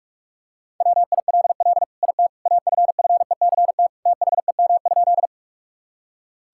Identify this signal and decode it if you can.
Morse code, heard as a keyed tone, about 41 wpm, 690 Hz. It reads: WILL IT AFFECT THEM?
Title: Sentences from Top 1000 Words - Review Part 6 - 45wpm